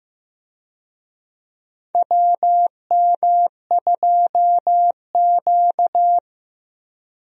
Morse code, heard as a keyed tone, about 15 wpm, 700 Hz. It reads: WM2Q
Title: USA Call Signs - 4 Characters - Repeated 2x - 15wpm